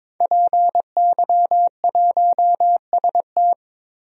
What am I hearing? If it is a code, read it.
Morse code, heard as a keyed tone, about 22 wpm, 700 Hz. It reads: PY1ST
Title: DXCC-100 Call Signs - 5 Characters - 22wpm